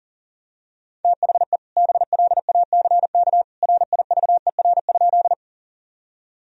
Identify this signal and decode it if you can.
Morse code, heard as a keyed tone, about 40 wpm, 700 Hz. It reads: THE BLACK RIVER?
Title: Sentences from Top 1000 Words - Review Part 2 - 40wpm